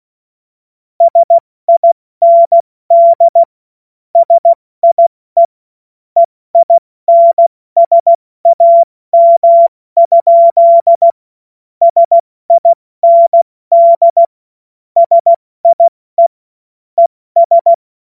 Morse code, heard as a keyed tone, about 16 wpm, 680 Hz. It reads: SIND SIE EINSAM? SIND SIE ES
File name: Morsekurs / Witze / Witz 24.Oktober 2023 70 F80